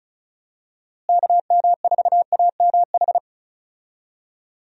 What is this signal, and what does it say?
Morse code, heard as a keyed tone, about 34 wpm, 700 Hz. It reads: KM4AMH
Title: USA Call Signs - 6 Characters - Repeated 3x - 35wpm